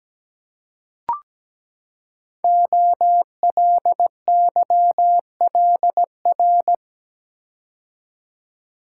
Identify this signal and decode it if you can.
Morse code, heard as a keyed tone, about 17 wpm, 700 Hz. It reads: OLYLR
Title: Code Groups - 5 Letter:Numbers - 17wpm